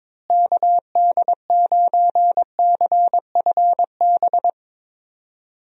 Morse code, heard as a keyed tone, about 21 wpm, 700 Hz. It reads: KD9CFB
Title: USA Call Signs - 6 Characters - Repeated 2x - 22wpm